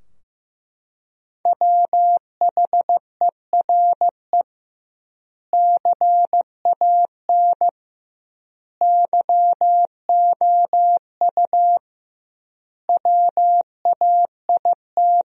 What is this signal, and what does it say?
Morse code, heard as a keyed tone, about 15 wpm, 700 Hz. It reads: WHERE CAN YOU WAIT
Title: 2x Word-Spacing - Sentences from Top 500 Words - 15wpm